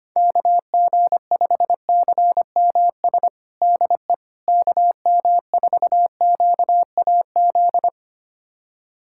Morse code, heard as a keyed tone, about 25 wpm, 700 Hz. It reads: KG5CMS DE KM4QAZ
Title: USA Callsign DE Callsign - 6 Characters - 25wpm